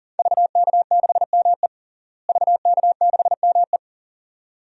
Morse code, heard as a keyed tone, about 40 wpm, 700 Hz. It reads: VK6ME VK6ME
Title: DXCC-100 Call Signs - 5 Characters - Repeated 2x - 40wpm